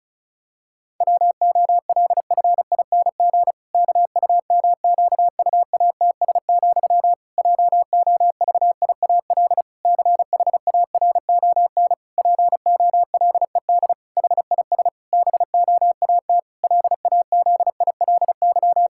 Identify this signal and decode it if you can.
Morse code, heard as a keyed tone, about 36 wpm, 700 Hz. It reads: WOLFING KUMQUATS, JOVIAL CHARON POLED HIS BOAT LAZILY
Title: Pangram Challenge v2 - 35wpm